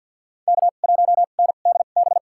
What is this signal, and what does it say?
Morse code, heard as a keyed tone, about 50 wpm, spaced as for 36 wpm, 700 Hz. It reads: K1NDB
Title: USA Call Signs - 5 Characters - 2x Character Spacing - 50wpm